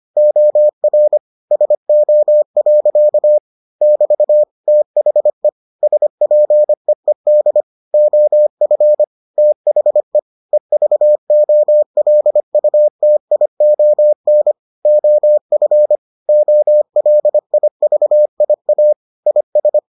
Morse code, heard as a keyed tone, about 25 wpm, 600 Hz. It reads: OR SO. <BT> THE SPEED OF THE EVOLUTION OF OLIVIA IS